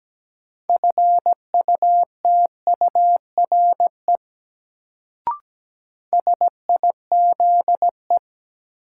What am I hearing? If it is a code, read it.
Morse code, heard as a keyed tone, about 17 wpm, 700 Hz. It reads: FUTURE SIZE
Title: Sets of 3 Words - New Word Drill - 17wpm